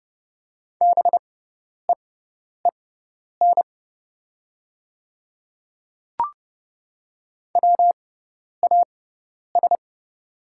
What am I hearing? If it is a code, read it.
Morse code, heard as a keyed tone, about 30 wpm, spaced as for 10 wpm, 700 Hz. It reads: BEEN WAS
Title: 6x Character Spacing - Top 200 Words - 30wpm